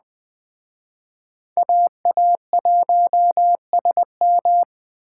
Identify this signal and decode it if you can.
Morse code, heard as a keyed tone, about 20 wpm, 700 Hz. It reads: AA1SM
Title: Speed Racing - USA 2x2 Call Signs 17wpm